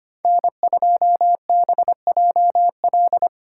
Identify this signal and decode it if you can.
Morse code, heard as a keyed tone, about 30 wpm, 700 Hz. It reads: N2BJL
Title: Speed Racing - USA Call Signs 25wpm